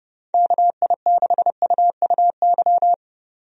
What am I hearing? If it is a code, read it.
Morse code, heard as a keyed tone, about 31 wpm, 700 Hz. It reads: KI6UUY
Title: USA Call Signs - 6 Characters - 30wpm